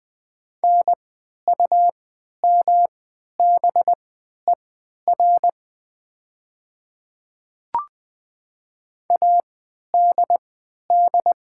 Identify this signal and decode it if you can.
Morse code, heard as a keyed tone, about 20 wpm, spaced as for 11 wpm, 700 Hz. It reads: NUMBER ADD
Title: 3x Character Spacing - Top 200 Words - 20wpm